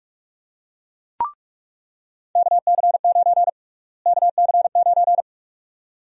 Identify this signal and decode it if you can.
Morse code, heard as a keyed tone, about 46 wpm, 700 Hz. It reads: KC9 KC9
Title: USA Call Sign Prefixes - Repeated 2x - 45wpm